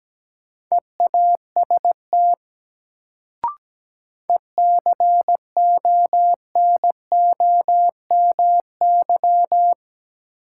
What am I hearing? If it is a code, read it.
Morse code, heard as a keyed tone, about 17 wpm, 700 Hz. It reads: EAST ECONOMY
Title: Sentences from Top 1000 Words - Review Part 3 - New Word Drill - 17wpm